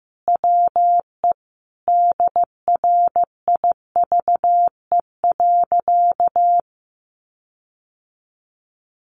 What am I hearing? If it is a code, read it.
Morse code, heard as a keyed tone, about 15 wpm, 700 Hz. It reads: WE DRIVE.
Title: Sentences from Top 1300 Words 15wpm